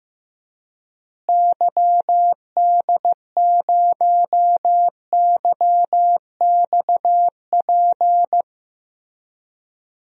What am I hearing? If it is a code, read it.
Morse code, heard as a keyed tone, about 15 wpm, 700 Hz. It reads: YD0YXP
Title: DXCC-100 Call Signs - 6 Characters - Repeated 2x - 15wpm